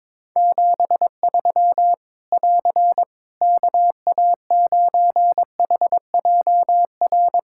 Morse code, heard as a keyed tone, about 22 wpm, 700 Hz. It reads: 73 <AR> KA9HJR
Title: Practice QSO - Exchange 4 - Ending - 22wpm